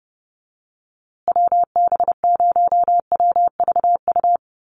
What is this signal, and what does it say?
Morse code, heard as a keyed tone, about 30 wpm, 700 Hz. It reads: WB0WVU
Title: USA Call Signs Repeated 2x 30wpm